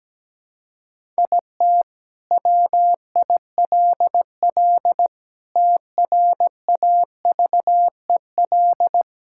Morse code, heard as a keyed tone, about 17 wpm, 700 Hz. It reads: IT WILL TRAVEL
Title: Sentences from Top 1000 Words - Review Part 6 - 17wpm